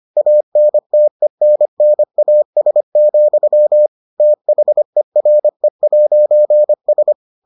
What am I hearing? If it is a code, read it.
Morse code, heard as a keyed tone, about 25 wpm, 600 Hz. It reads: ANTENNAS, THERE'S